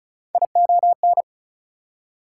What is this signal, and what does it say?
Morse code, heard as a keyed tone, about 36 wpm, 700 Hz. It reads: ION
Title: Prefixes and Suffixes 35wpm